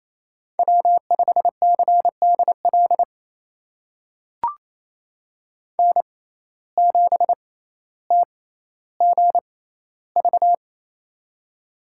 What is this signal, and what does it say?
Morse code, heard as a keyed tone, about 28 wpm, spaced as for 14 wpm, 700 Hz. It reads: W5CDL N7TGV
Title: USA Call Signs - 5 Characters - 6x Character Spacing - 28wpm